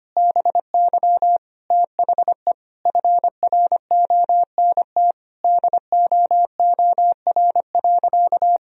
Morse code, heard as a keyed tone, about 25 wpm, 700 Hz. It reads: BY THE FRONT DOOR.
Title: Sentences from Top 1000 Words - Review Part 2 - 25wpm